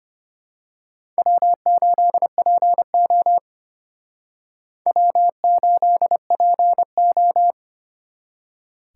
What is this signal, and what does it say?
Morse code, heard as a keyed tone, about 29 wpm, 700 Hz. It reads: W8PO W8PO
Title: Speed Racing - USA 1x2 Call Signs 25wpm